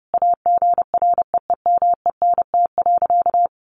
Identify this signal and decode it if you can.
Morse code, heard as a keyed tone, about 30 wpm, 700 Hz. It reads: AGREEMENT.
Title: Sentences from Top 1500 Words 30wpm